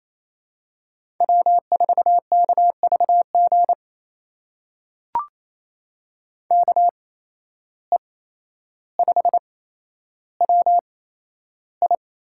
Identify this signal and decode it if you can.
Morse code, heard as a keyed tone, about 28 wpm, spaced as for 12 wpm, 700 Hz. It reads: W4KVG KE5WI
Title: USA Call Signs - 5 Characters - 8x Character Spacing - 28wpm